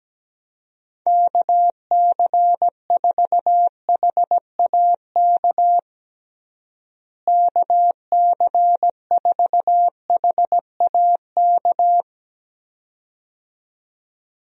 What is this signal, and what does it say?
Morse code, heard as a keyed tone, about 17 wpm, 700 Hz. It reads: KC4HAK KC4HAK
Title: USA Call Signs - 6 Characters - Repeated 2x - 17wpm